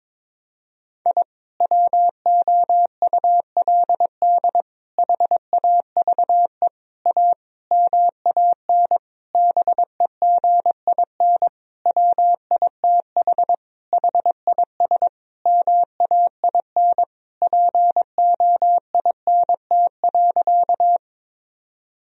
Morse code, heard as a keyed tone, about 22 wpm, 700 Hz. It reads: I WOULD HAVE A MAN BEGIN WITH HIS MAIN POINT.
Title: Sentences from Top 1000 Words - Review Part 2 - 22wpm